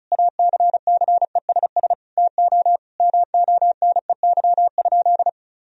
Morse code, heard as a keyed tone, about 34 wpm, 700 Hz. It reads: ACCESS TO MONEY?
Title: Sentences from Top 1000 Words - Review Part 4 - 35wpm